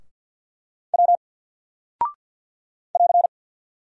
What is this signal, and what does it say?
Morse code, heard as a keyed tone, about 48 wpm, 700 Hz. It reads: W <AR>
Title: Single Character (Letter:Number:Punctuation:Prosign) 50wpm